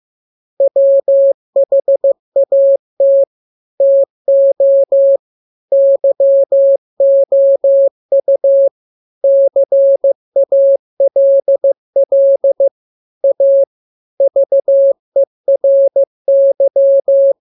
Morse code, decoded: WHAT TO YOU CALL A VERY